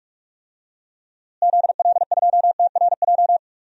Morse code, heard as a keyed tone, about 46 wpm, 700 Hz. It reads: ZL2TRJ
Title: DXCC-100 Call Signs - 6 Characters - Repeated 2x - 45wpm